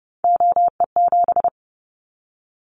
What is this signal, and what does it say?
Morse code, heard as a keyed tone, about 30 wpm, 700 Hz. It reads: OE7
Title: Callsign Prefixes for Top 50 DXCC - 30wpm